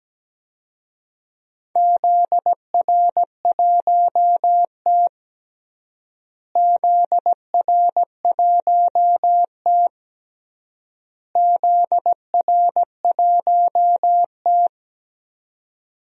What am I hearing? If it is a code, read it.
Morse code, heard as a keyed tone, about 17 wpm, 700 Hz. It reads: ZR1T ZR1T ZR1T
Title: DXCC-100 Call Signs - 4 Characters - Repeated 3x - 17wpm